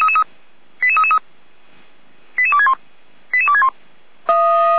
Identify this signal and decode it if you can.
Five-tone selective call (ZVEI), heard as quick runs of five tones, tones of about 70 ms. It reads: 88333 88261 88261